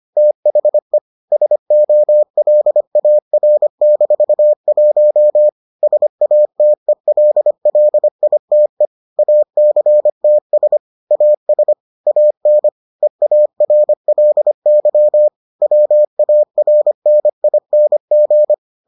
Morse code, heard as a keyed tone, about 25 wpm, 600 Hz. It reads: THE SOLAR-1 SATELLITE ACTS AS AN EARLY WARNING